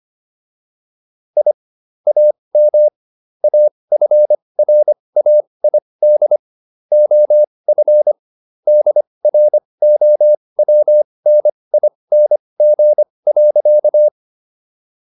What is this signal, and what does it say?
Morse code, heard as a keyed tone, about 25 wpm, spaced as for 20 wpm, 600 Hz. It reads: I AM AFRAID OF DROWNING.